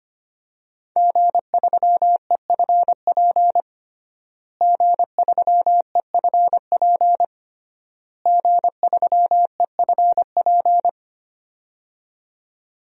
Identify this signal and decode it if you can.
Morse code, heard as a keyed tone, about 25 wpm, 700 Hz. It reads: G3EFP G3EFP G3EFP
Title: DXCC-100 Call Signs - 5 Characters - Repeated 3x - 25wpm